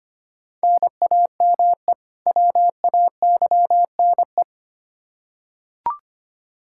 Morse code, heard as a keyed tone, about 25 wpm, 700 Hz. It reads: NAME WAYNE
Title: Practice QSO - Exchange 1 - Beginning - 25wpm